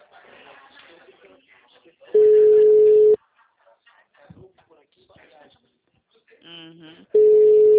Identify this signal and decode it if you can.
Ringback tone (Europe, 425 Hz), with speech.